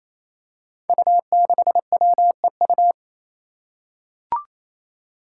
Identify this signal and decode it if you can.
Morse code, heard as a keyed tone, about 28 wpm, 700 Hz. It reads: U6WEU
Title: Five Challenging Letter-Number Combinations 28wpm